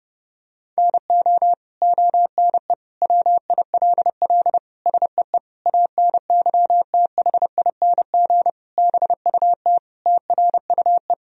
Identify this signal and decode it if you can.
Morse code, heard as a keyed tone, about 32 wpm, 700 Hz. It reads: NO ONE WILL SEE ANYTHING BUT TRUE
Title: Sentences from Top 1000 Words - Review Part 3 - 30wpm